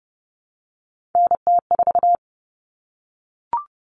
Morse code, heard as a keyed tone, about 30 wpm, 700 Hz. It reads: NT4
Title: All Cut Numbers - 3-Digit 30wpm